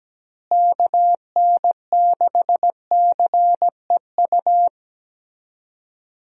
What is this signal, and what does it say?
Morse code, heard as a keyed tone, about 17 wpm, 700 Hz. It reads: KN6CEU
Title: USA Call Signs - 6 Characters - Repeated 2x - 17wpm